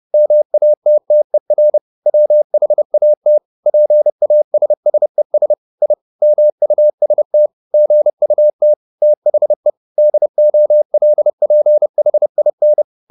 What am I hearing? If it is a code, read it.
Morse code, heard as a keyed tone, about 30 wpm, 600 Hz. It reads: MATTER WHAT PASSES I MUST GUT THE DOLPHIN